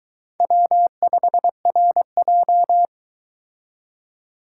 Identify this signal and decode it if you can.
Morse code, heard as a keyed tone, about 22 wpm, 700 Hz. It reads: W5RJ